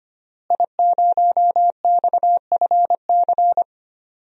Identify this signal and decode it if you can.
Morse code, heard as a keyed tone, about 25 wpm, 700 Hz. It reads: I0XFC
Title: DXCC-100 Call Signs - 5 Characters - Repeated 3x - 25wpm